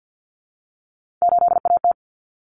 Morse code, heard as a keyed tone, about 50 wpm, 700 Hz. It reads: 8AT
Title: Common Cut Numbers - 3-Digit 50wpm